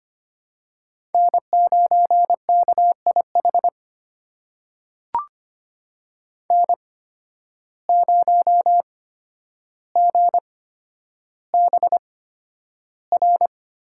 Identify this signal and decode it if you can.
Morse code, heard as a keyed tone, about 25 wpm, spaced as for 11 wpm, 700 Hz. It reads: N9KIH N0GBR
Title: USA Call Signs - 5 Characters - 8x Character Spacing - 25wpm